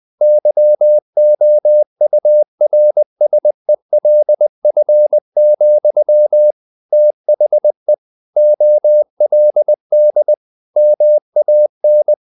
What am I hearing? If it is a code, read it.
Morse code, heard as a keyed tone, about 20 wpm, 600 Hz. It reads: YOURSELF, THE OLD MAN
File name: the old man and the sea / the old man and the sea part 1 20wpm